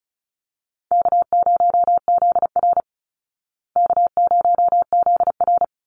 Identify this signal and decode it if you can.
Morse code, heard as a keyed tone, about 34 wpm, 700 Hz. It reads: K0ZR K0ZR
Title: Call Signs Repeated 3x - 35wpm